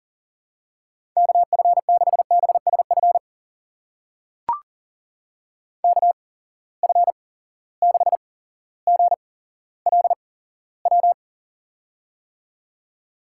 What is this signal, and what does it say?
Morse code, heard as a keyed tone, about 38 wpm, spaced as for 17 wpm, 700 Hz. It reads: KF6BSF KF6GLW
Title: USA Call Signs - 6 Characters - 8x Character Spacing - 40wpm